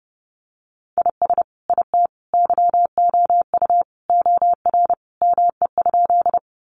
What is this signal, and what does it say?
Morse code, heard as a keyed tone, about 30 wpm, 700 Hz. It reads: IS IT YOU OR ME?
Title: Sentences from Top 200 Words 30wpm